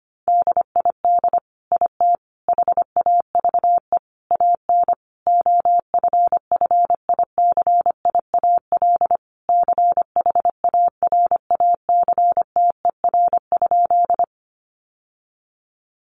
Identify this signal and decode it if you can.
Morse code, heard as a keyed tone, about 25 wpm, 700 Hz. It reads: DID IT HAVE AN OFFICIAL CHARACTER?
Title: Sentences from Top 1200 Words 25wpm